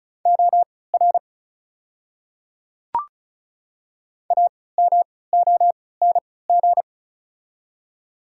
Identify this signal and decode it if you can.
Morse code, heard as a keyed tone, about 34 wpm, spaced as for 20 wpm, 690 Hz. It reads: OR AMONG
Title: 3x Character Spacing - Top 500 Words - 35wpm